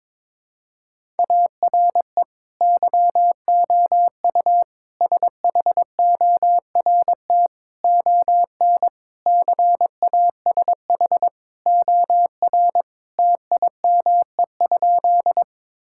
Morse code, heard as a keyed tone, about 22 wpm, 700 Hz. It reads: ARE YOU SHORT ON CASH OR TIME?
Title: Sentences from Top 1000 Words - Review Part 5 - 22wpm